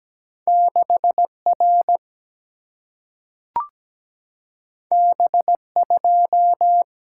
Morse code, heard as a keyed tone, about 17 wpm, 700 Hz. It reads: 6R B2
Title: Single-Letter Single-Number 17wpm